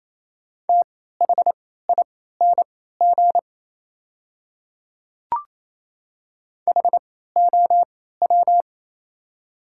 Morse code, heard as a keyed tone, about 28 wpm, spaced as for 16 wpm, 700 Hz. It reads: THING HOW
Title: 3x Character Spacing - Top 300 Words - 28wpm